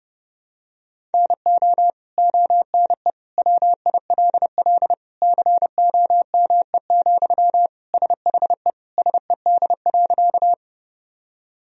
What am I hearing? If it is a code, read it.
Morse code, heard as a keyed tone, about 30 wpm, 700 Hz. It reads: NO ONE WILL COME, SHE SED.
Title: Sentences from Top 100 Words and Abbreviations 30wpm